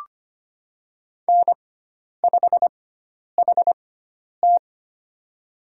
Morse code, heard as a keyed tone, about 25 wpm, spaced as for 10 wpm, 700 Hz. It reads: N5HT
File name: Lesson 16g - Letter c - Call Signs 25wpm